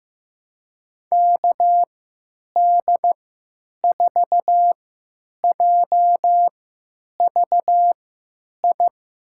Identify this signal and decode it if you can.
Morse code, heard as a keyed tone, about 15 wpm, spaced as for 9 wpm, 700 Hz. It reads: KD4JVI
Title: USA Call Signs - 6 Characters - 3x Character Spacing - 15wpm